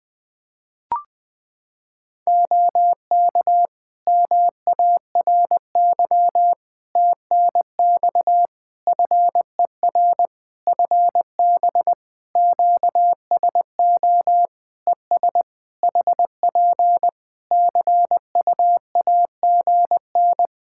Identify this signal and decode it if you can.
Morse code, heard as a keyed tone, about 20 wpm, 700 Hz. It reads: OK MARY TNX FER FB QSO ES HP CUAGN
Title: Practice QSO - Exchange 4 - Ending - 20wpm